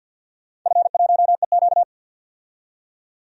Morse code, heard as a keyed tone, about 53 wpm, 700 Hz. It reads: F1EQ
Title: DXCC-100 Call Signs - 4 Characters - 50wpm